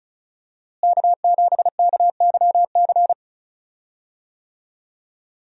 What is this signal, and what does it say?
Morse code, heard as a keyed tone, about 34 wpm, 700 Hz. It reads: K7KYC
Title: USA Call Signs - 5 Characters - Repeated 2x - 35wpm